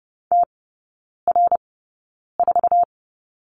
Morse code, heard as a keyed tone, about 30 wpm, spaced as for 9 wpm, 700 Hz. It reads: TR4
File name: Lesson 16e - Letter c - Three Characters 30wpm